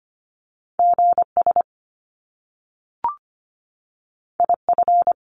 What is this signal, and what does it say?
Morse code, heard as a keyed tone, about 25 wpm, 700 Hz. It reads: GS IF
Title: Top Two Letter Combinations in Words 25wpm